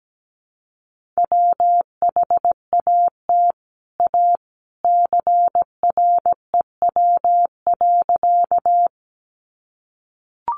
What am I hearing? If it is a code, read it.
Morse code, heard as a keyed tone, about 17 wpm, 700 Hz. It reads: WHAT A CREW.
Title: Sentences from Top 1500 Words 17wpm